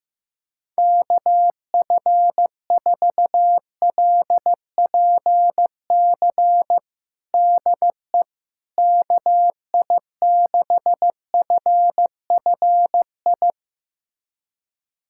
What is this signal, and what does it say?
Morse code, heard as a keyed tone, about 16 wpm, 700 Hz. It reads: KF4LPC DE KI6FFI
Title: USA Callsign DE Callsign - 6 Characters - 15wpm